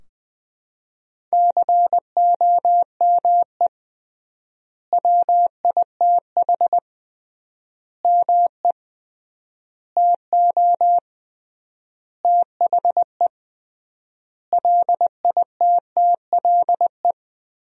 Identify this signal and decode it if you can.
Morse code, heard as a keyed tone, about 20 wpm, 700 Hz. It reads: COME WITH ME TO THE LITTLE
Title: 3x Word-Spacing - Sentences from Top 200 Words - 20wpm